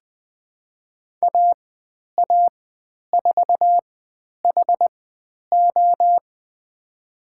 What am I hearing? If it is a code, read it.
Morse code, heard as a keyed tone, about 20 wpm, spaced as for 10 wpm, 700 Hz. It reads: AA4HO
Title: 102 Lesson 17g - Letter u - Call Signs 20wpm